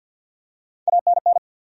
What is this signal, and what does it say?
Morse code, heard as a keyed tone, about 53 wpm, 700 Hz. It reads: ANN
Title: USA Common Names - Max 3 Letters - 50wpm